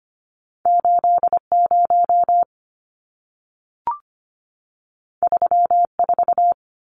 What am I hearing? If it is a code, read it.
Morse code, heard as a keyed tone, about 25 wpm, 700 Hz. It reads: 80 34
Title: Numbers - 2-Digit 25wpm